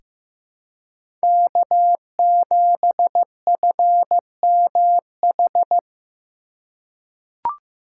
Morse code, heard as a keyed tone, about 15 wpm, 700 Hz. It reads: K7FMH
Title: USA Call Signs - 5 Characters - 6x Character Spacing - 15wpm